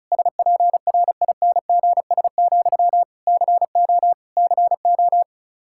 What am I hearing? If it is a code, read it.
Morse code, heard as a keyed tone, about 34 wpm, 700 Hz. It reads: SPRINGS, CO CO
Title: K1USN SST QSO Practice 35wpm